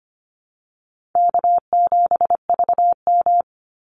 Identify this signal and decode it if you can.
Morse code, heard as a keyed tone, about 25 wpm, 700 Hz. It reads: K7VM
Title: Easy Call Signs - 25wpm